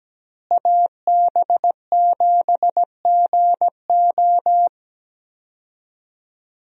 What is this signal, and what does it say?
Morse code, heard as a keyed tone, about 17 wpm, 700 Hz. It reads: AB7GO
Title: USA Call Signs - 5 Characters - Repeated 2x - 17wpm